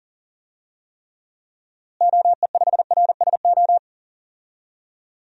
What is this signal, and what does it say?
Morse code, heard as a keyed tone, about 37 wpm, 700 Hz. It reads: OE5RSO
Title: DXCC-100 Call Signs - 6 Characters - Repeated 2x - 40wpm